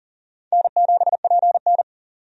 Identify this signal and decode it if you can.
Morse code, heard as a keyed tone, about 40 wpm, 700 Hz. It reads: N7PN
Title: USA Call Signs - 4 Characters - 40wpm